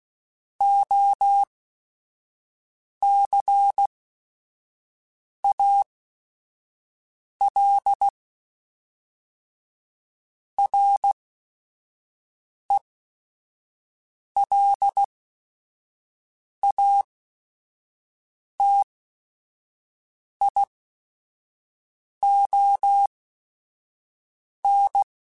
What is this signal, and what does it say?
Morse code, heard as a keyed tone, about 16 wpm, spaced as for 5 wpm, 800 Hz. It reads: OCAL RELATION
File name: Technical Subjects / Technical Subjects-mp3s / XC @5